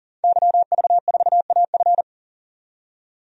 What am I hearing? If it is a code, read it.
Morse code, heard as a keyed tone, about 40 wpm, 700 Hz. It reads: YV4AF
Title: DXCC-100 Call Signs - 5 Characters - 40wpm